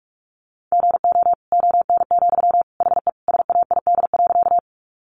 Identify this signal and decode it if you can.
Morse code, heard as a keyed tone, about 44 wpm, 700 Hz. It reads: GO ON, HE SAID.
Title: Sentences from Top 100 Words 45wpm